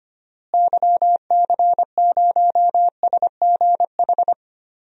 Morse code, heard as a keyed tone, about 25 wpm, 700 Hz. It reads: YC0SGH
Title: DXCC-100 Call Signs - 6 Characters - Repeated 3x - 25wpm